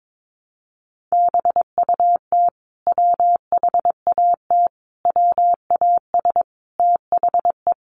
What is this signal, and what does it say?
Morse code, heard as a keyed tone, about 22 wpm, 700 Hz. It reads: BUT WHAT WAS THE
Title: Sentences from Top 1100 Words 22wpm